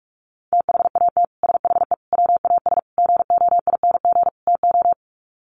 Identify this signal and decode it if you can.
Morse code, heard as a keyed tone, about 44 wpm, 690 Hz. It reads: THAT SHE WAS GOING TO
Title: Sets of 5 Words Encore2 45wpm